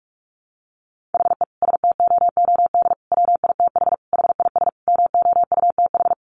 Morse code, heard as a keyed tone, about 44 wpm, 700 Hz. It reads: HE STOOD WITH HIS MOUTH
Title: Sentences from Top 1500 Words 45wpm